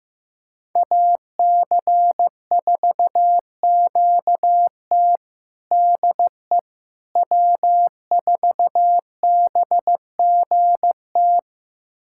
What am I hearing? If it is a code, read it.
Morse code, heard as a keyed tone, about 15 wpm, 700 Hz. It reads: AC4QT DE W4BGT
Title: USA Callsign DE Callsign - 5 Characters - 15wpm